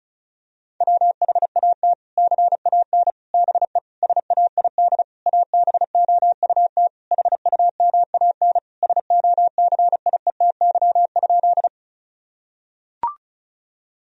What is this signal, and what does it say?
Morse code, heard as a keyed tone, about 36 wpm, 700 Hz. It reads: WHAT CAN BE SAID ABOUT HUMAN SOCIETY?
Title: Sentences from Top 1000 Words - Review Part 4 - 35wpm